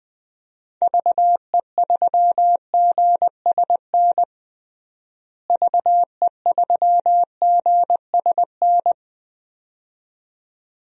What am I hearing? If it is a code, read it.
Morse code, heard as a keyed tone, about 20 wpm, 700 Hz. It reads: VE3GSN VE3GSN
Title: DXCC-100 Call Signs - 6 Characters - Repeated 2x - 20wpm